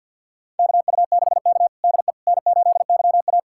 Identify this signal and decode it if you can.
Morse code, heard as a keyed tone, about 52 wpm, 690 Hz. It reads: KU6K DE N8YA
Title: USA Callsign DE Callsign - 4 Characters - 50wpm